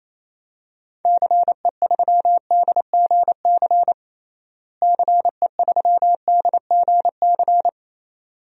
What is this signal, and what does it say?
Morse code, heard as a keyed tone, about 28 wpm, 700 Hz. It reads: CE3DGC CE3DGC
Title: DXCC-100 Call Signs - 6 Characters - Repeated 2x - 28wpm